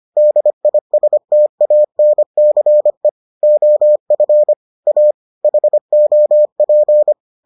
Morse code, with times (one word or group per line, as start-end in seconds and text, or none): 0.2-3.1 DISTANCE
3.4-4.5 OF
4.9-5.1 A
5.4-7.1 HOP